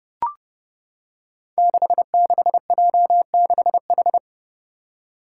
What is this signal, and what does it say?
Morse code, heard as a keyed tone, about 30 wpm, 700 Hz. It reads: 66J6H
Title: Five Challenging Letter-Number Combinations 30wpm